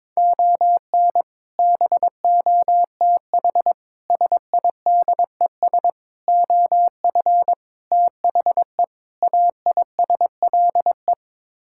Morse code, heard as a keyed tone, about 22 wpm, 700 Hz. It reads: ON BOTH SIDES OF THE AISLE
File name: practice / 22wpm / Sets of 6 Words - 22wpm